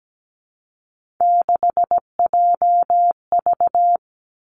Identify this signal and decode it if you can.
Morse code, heard as a keyed tone, about 17 wpm, 700 Hz. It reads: 6JV
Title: Three Challenging Letter-Number Combinations 17wpm